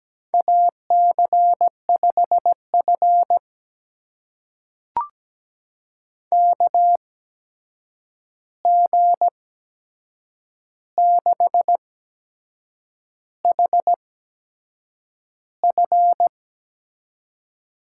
Morse code, heard as a keyed tone, about 17 wpm, spaced as for 7 wpm, 700 Hz. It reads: AC5F KG6HF
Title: USA Call Signs - 8x Character Spacing - 17wpm